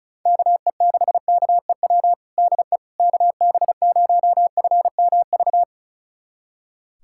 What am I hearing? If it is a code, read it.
Morse code, heard as a keyed tone, about 36 wpm, 700 Hz. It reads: KE6KEW DE KB0FMV
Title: USA Callsign DE Callsign 35wpm